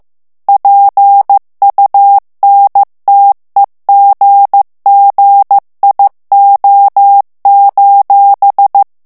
Morse code, heard as a keyed tone, about 15 wpm, 800 Hz. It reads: PUNTEGGIO:
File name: morse mp3 / ITALIEN texte clair 2 vitesse 14 mots minute